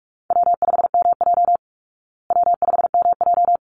Morse code, heard as a keyed tone, about 44 wpm, 700 Hz. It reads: W5MJ W5MJ
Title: Easy Call Signs Repeated 2x - 45wpm